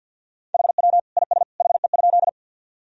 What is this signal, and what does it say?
Morse code, heard as a keyed tone, about 52 wpm, 700 Hz. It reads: HW IS HE?
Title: Sentences from Top 100 Words and Abbreviations 50wpm